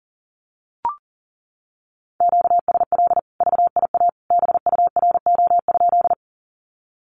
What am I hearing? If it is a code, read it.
Morse code, heard as a keyed tone, about 40 wpm, 700 Hz. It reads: QSL VIA BURO?
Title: Common QSO Phrases 40wpm